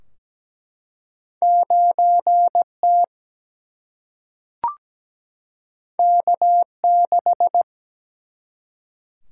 Morse code, heard as a keyed tone, about 17 wpm, 700 Hz. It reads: 9T K6
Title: Single-Letter Single-Number 17wpm